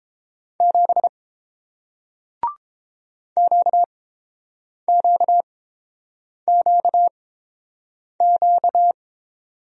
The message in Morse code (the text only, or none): 7 QQQQ